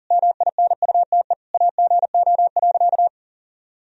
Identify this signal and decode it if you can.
Morse code, heard as a keyed tone, about 40 wpm, 700 Hz. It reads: MINUTE AGO.
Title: Sentences from Top 500 Words and Abbreviations 40wpm